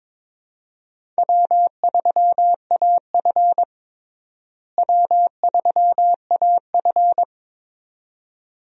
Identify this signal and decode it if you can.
Morse code, heard as a keyed tone, about 21 wpm, 700 Hz. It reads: W3AF W3AF
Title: USA Call Signs - 4 Characters - Repeated 2x - 22wpm